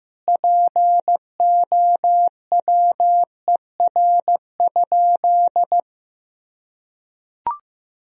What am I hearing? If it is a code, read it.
Morse code, heard as a keyed tone, about 16 wpm, 700 Hz. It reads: POWER?
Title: Sentences from Top 1000 Words - Review Part 2 - 15wpm